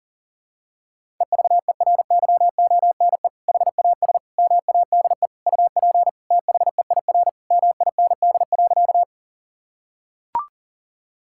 Morse code, heard as a keyed tone, about 40 wpm, 700 Hz. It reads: EVERYONE HAS MADE UP THEIR MIND.
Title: Sentences from Top 1000 Words - Review Part 5 - 40wpm